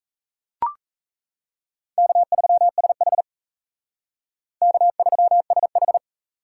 Morse code, heard as a keyed tone, about 35 wpm, 700 Hz. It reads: K3SH K3SH